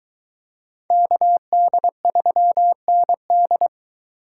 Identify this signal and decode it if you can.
Morse code, heard as a keyed tone, about 22 wpm, 700 Hz. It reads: KD3ND